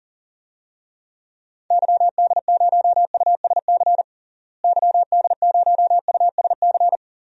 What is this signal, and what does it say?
Morse code, heard as a keyed tone, about 40 wpm, 700 Hz. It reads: YD0USC YD0USC
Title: DXCC-100 Call Signs - 6 Characters - Repeated 2x - 40wpm